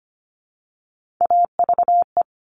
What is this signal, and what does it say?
Morse code, heard as a keyed tone, about 25 wpm, 700 Hz. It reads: AVE
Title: Top Three Letter Combinations in Words 25wpm